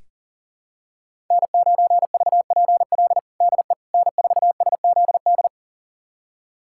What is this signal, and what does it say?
Morse code, heard as a keyed tone, about 40 wpm, 700 Hz. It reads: N9VPL DE N4SZD
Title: USA Callsign DE Callsign - 5 Characters - 40wpm